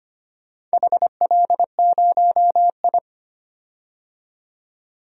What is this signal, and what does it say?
Morse code, heard as a keyed tone, about 25 wpm, 700 Hz. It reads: HL0I
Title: DXCC-100 Call Signs - 4 Characters - 25wpm